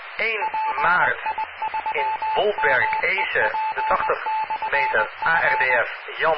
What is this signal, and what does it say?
Morse code, heard as a keyed tone, about 20 wpm, 870 Hz, with speech and radio noise. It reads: PI3MEP/F